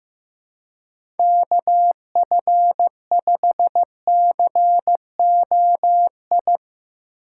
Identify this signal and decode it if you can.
Morse code, heard as a keyed tone, about 15 wpm, 700 Hz. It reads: KF5COI
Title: USA Call Signs - 6 Characters - 4x Character Spacing - 15wpm